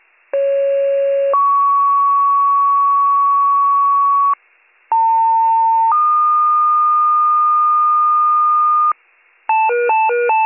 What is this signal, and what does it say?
Two-tone page, heard as a one-second tone, then a three-second tone, about 550 then 1120 Hz, about 900 then 1220 Hz. It is heard from 0.3 to 8.9 s.